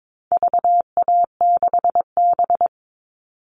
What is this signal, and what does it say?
Morse code, heard as a keyed tone, about 22 wpm, 700 Hz. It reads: VA6B
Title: All Cut Numbers - 4-Digit 22wpm